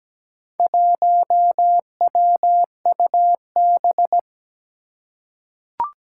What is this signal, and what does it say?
Morse code, heard as a keyed tone, about 17 wpm, 700 Hz. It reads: 1WUB
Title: Four Challenging Letter-Number Combinations 17wpm